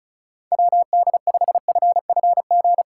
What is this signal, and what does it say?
Morse code, heard as a keyed tone, about 34 wpm, 700 Hz. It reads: WD5FFG
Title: USA Call Signs - 6 Characters - 2x Character Spacing - 35wpm